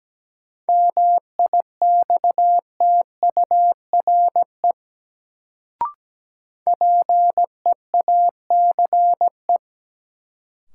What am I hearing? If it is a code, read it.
Morse code, heard as a keyed tone, about 17 wpm, 700 Hz. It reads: MIXTURE PEACE
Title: Sentences from Top 1500 Words - New Word Drill - 17wpm